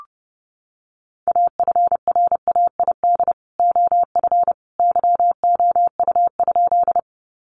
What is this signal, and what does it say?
Morse code, heard as a keyed tone, about 30 wpm, 700 Hz. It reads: AFRAID OF YOU?
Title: Sentences from Top 1500 Words 30wpm